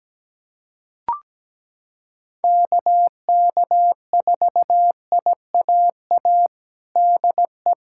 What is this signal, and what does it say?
Morse code, heard as a keyed tone, about 17 wpm, 700 Hz. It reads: KK4IAA DE
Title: USA Callsign DE Callsign - 6 Characters - 17wpm